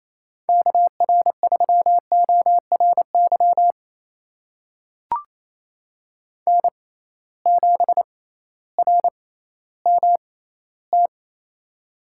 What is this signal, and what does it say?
Morse code, heard as a keyed tone, about 28 wpm, spaced as for 15 wpm, 700 Hz. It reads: KR3ORY N7RMT
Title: USA Call Signs - 6x Character Spacing - 28wpm